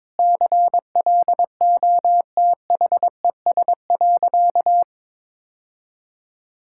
Morse code, heard as a keyed tone, about 22 wpm, 700 Hz. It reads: CLOTHES.
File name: practice / 22wpm / Easy Sentences with Hints - 22wpm